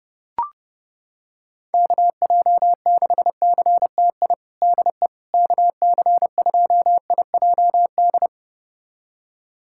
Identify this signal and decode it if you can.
Morse code, heard as a keyed tone, about 30 wpm, 700 Hz. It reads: KJ6CTI DE KC2IJD
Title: USA Callsign DE Callsign - 6 Characters - 30wpm